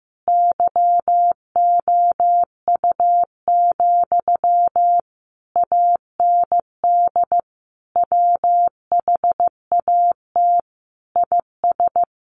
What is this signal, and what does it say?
Morse code, heard as a keyed tone, about 15 wpm, 700 Hz. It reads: YOU, AND WHAT IS
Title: Sentences from Top 1200 Words 15wpm